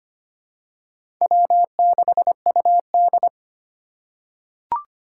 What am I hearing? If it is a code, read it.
Morse code, heard as a keyed tone, about 25 wpm, 700 Hz. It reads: W6UD
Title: USA Call Signs - 4 Characters - 3x Character Spacing - 25wpm